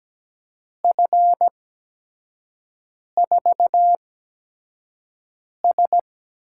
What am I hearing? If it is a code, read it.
Morse code, heard as a keyed tone, about 17 wpm, spaced as for 5 wpm, 700 Hz. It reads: F4S
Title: USA Call Signs - 5 Characters - 8x Character Spacing - 17wpm